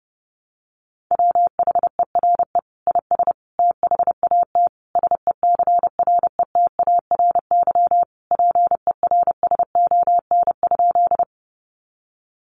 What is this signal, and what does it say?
Morse code, heard as a keyed tone, about 28 wpm, 700 Hz. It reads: WHERE IS THAT SECRETARY PERSON?
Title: Sentences from Top 1300 Words 30wpm